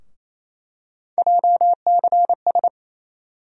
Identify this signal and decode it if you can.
Morse code, heard as a keyed tone, about 28 wpm, 700 Hz. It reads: JCS
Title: USA Call Sign Postfixes - 3 Letters 28wpm